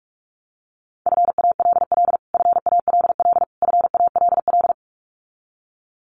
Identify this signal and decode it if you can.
Morse code, heard as a keyed tone, about 44 wpm, 700 Hz. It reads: FALL FALL FALL
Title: Top 500 Words Repeated 45wpm